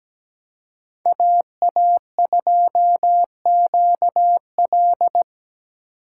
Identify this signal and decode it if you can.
Morse code, heard as a keyed tone, about 17 wpm, 700 Hz. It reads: AA2QL
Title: USA Call Signs - 5 Characters - 2x Character Spacing - 17wpm